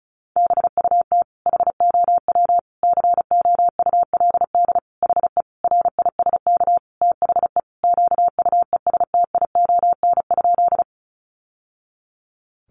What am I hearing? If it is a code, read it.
Morse code, heard as a keyed tone, about 34 wpm, 700 Hz. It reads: BUT HOW COULD HE RISK THE QUESTION?
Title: Sentences from Top 600 Words 35wpm